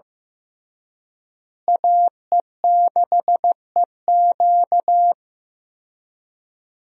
Morse code, heard as a keyed tone, about 15 wpm, 700 Hz. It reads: AE6EQ
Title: USA Call Signs - 5 Characters - Repeated 3x - 15wpm